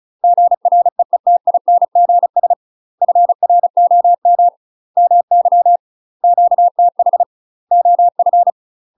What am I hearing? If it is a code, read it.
Morse code, heard as a keyed tone, about 36 wpm, 700 Hz. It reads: GREETINGS FROM MY QTH OF
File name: UK MASTER QSOs TRAINING / 35WPM / UK MASTER QSOs 0043